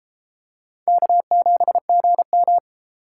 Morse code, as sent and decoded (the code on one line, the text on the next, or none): -.- --... --. --
K7GM